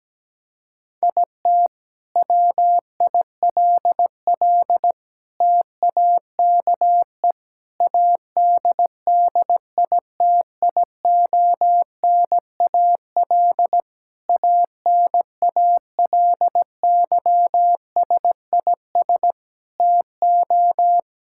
Morse code, heard as a keyed tone, about 17 wpm, 700 Hz. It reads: IT WILL TAKE ADDITIONAL ANALYSIS TO
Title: Sentences from Top 1000 Words - Review Part 5 - 17wpm